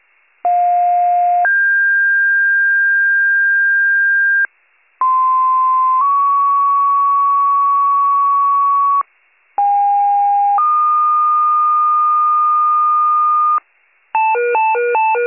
Two-tone page, heard as a one-second tone, then a three-second tone, about 710 then 1600 Hz, about 1040 then 1120 Hz, about 810 then 1220 Hz. It is heard from 0.4 to 13.6 s.